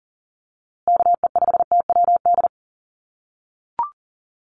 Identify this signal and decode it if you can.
Morse code, heard as a keyed tone, about 37 wpm, 700 Hz. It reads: KE5TWD
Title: USA Call Signs Repeated 3x 40wpm